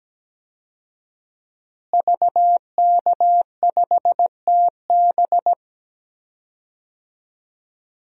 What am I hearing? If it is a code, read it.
Morse code, heard as a keyed tone, about 17 wpm, 700 Hz. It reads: VK5TB
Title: DXCC-100 Call Signs - 5 Characters - Repeated 3x - 17wpm